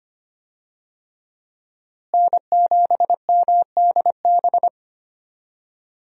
Morse code, heard as a keyed tone, about 25 wpm, 700 Hz. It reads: N7MDB
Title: USA Call Signs - 5 Characters - Repeated 3x - 25wpm